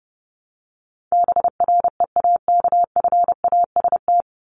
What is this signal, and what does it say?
Morse code, heard as a keyed tone, about 28 wpm, 700 Hz. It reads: BREAKFAST
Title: Top 1251-1500 Words 30wpm